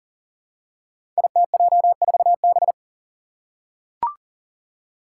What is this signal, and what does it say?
Morse code, heard as a keyed tone, about 41 wpm, 700 Hz. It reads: ITJ4B
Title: Five Challenging Letter-Number Combinations 40wpm